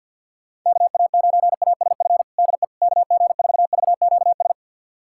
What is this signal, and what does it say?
Morse code, heard as a keyed tone, about 51 wpm, 690 Hz. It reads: KA9ASF DE KG4VQS
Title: USA Callsign DE Callsign 50wpm